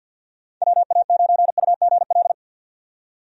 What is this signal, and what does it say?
Morse code, heard as a keyed tone, about 51 wpm, 700 Hz. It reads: WA9UGL
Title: USA Call Signs - 6 Characters - 50wpm